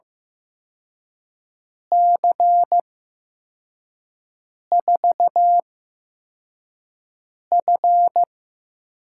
Morse code, heard as a keyed tone, about 15 wpm, spaced as for 4 wpm, 700 Hz. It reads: C4F
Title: USA Call Signs - 6 Characters - 8x Character Spacing - 15wpm